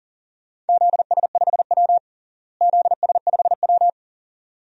Morse code, heard as a keyed tone, about 40 wpm, 700 Hz. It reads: ZS5W ZS5W
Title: DXCC-100 Call Signs - 4 Characters - Repeated 2x - 40wpm